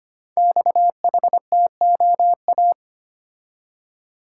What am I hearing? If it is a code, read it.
Morse code, heard as a keyed tone, about 25 wpm, 700 Hz. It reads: XHTOA
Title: Code Groups - 5 Letters - 25wpm